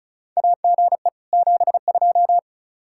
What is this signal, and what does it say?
Morse code, heard as a keyed tone, about 34 wpm, 700 Hz. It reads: AGE 72
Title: Practice QSO - Exchange 3 - More Info - 35wpm